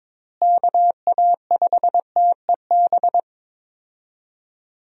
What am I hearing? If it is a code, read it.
Morse code, heard as a keyed tone, about 22 wpm, 700 Hz. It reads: KA5TEB
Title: USA Call Signs - 6 Characters - Repeated 2x - 22wpm